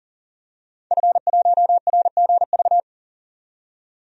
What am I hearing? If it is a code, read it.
Morse code, heard as a keyed tone, about 40 wpm, 700 Hz. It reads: F1RGV
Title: DXCC-100 Call Signs - 5 Characters - 40wpm